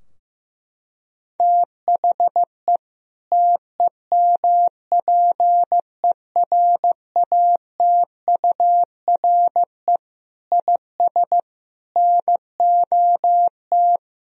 Morse code, heard as a keyed tone, about 15 wpm, 700 Hz. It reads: THE TEMPERATURE IS NOT
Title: Sentences from Top 1000 Words - Review Part 4 - 15wpm